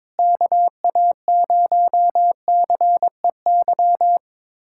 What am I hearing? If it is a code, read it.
Morse code, heard as a keyed tone, about 22 wpm, 700 Hz. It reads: KA0CEY
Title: USA Callsign DE Callsign 22wpm